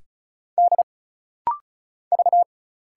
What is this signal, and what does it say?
Morse code, heard as a keyed tone, about 35 wpm, 700 Hz. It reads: D V
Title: Single Letter-Number Alt - Rapid-Fire 35wpm